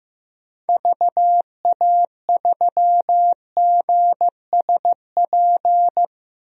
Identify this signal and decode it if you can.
Morse code, heard as a keyed tone, about 16 wpm, 700 Hz. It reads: VA3GSP
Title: DXCC-100 Call Signs - 6 Characters - 15wpm